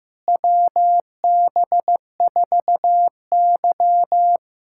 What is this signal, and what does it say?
Morse code, heard as a keyed tone, about 15 wpm, 700 Hz. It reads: WB4Y
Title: USA Callsign DE Callsign - 4 Characters - 15wpm